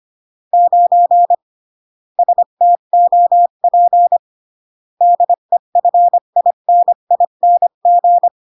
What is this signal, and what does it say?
Morse code, heard as a keyed tone, about 25 wpm, 700 Hz. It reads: 9 STOP DEFINING
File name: CW / jodorowsky / 82 commandments 25-20